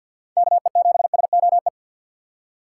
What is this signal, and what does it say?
Morse code, heard as a keyed tone, about 52 wpm, 700 Hz. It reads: KE7SOE
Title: USA Call Signs - 6 Characters - 3x Character Spacing - 50wpm